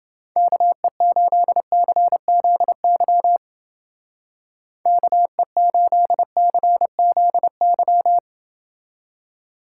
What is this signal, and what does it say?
Morse code, heard as a keyed tone, about 28 wpm, 700 Hz. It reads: KE8CZY KE8CZY